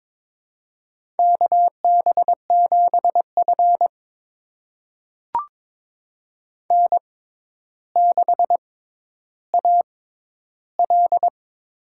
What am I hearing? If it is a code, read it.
Morse code, heard as a keyed tone, about 22 wpm, spaced as for 11 wpm, 700 Hz. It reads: KB7F N6AL
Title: USA Call Signs - 4 Characters - 6x Character Spacing - 22wpm